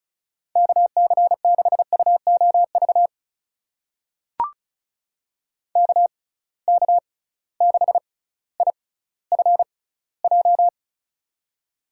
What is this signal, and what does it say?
Morse code, heard as a keyed tone, about 36 wpm, spaced as for 18 wpm, 700 Hz. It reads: KC6UOV KK6IFJ
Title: USA Call Signs - 6 Characters - 6x Character Spacing - 35wpm